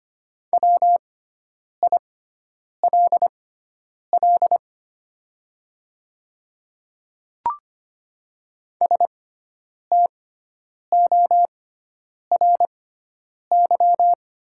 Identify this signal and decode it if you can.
Morse code, heard as a keyed tone, about 25 wpm, spaced as for 9 wpm, 700 Hz. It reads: WILL STORY
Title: 6x Character Spacing - Top 500 Words - 25wpm